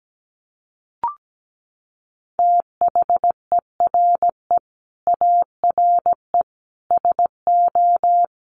Morse code, heard as a keyed tone, about 17 wpm, 700 Hz. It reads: THERE ARE SO
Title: Sets of 4 Words 17wpm